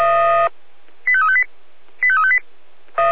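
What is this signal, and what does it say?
Five-tone selective call (ZVEI), heard as quick runs of five tones, tones of about 70 ms. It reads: 85358 85358
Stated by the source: Five-tone ZVEI selective call received on SDR.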